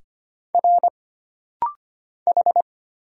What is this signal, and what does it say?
Morse code, heard as a keyed tone, about 25 wpm, 700 Hz. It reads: R H
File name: Single Character (Letter:Number:Punctuation:Prosign) - Rapid-Fire - 25wpm